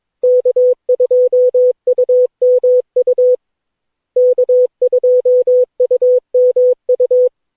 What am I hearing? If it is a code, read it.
Morse code, heard as a keyed tone, about 21 wpm, 500 Hz. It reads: K2UMU K2UMU
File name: cw / Session9 cw training